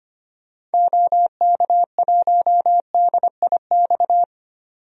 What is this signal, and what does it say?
Morse code, heard as a keyed tone, about 25 wpm, 700 Hz. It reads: OK1DIX
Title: DXCC-100 Call Signs - 6 Characters - 25wpm